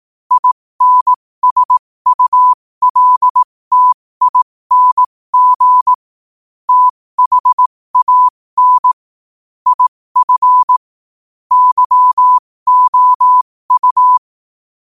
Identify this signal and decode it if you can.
Morse code, heard as a keyed tone, about 18 wpm, spaced as for 15 wpm, 1000 Hz. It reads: INSULTING THAN IF YOU
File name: qotdfiles / 15wpm